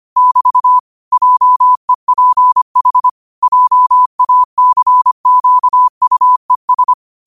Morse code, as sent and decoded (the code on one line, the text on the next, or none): -...- / .--- . .--. .... / .--- .- -.-. --.- ..- . ...
<BT> JEPH JACQUES